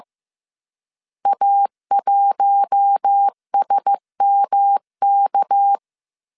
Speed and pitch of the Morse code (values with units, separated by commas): 15 wpm, 800 Hz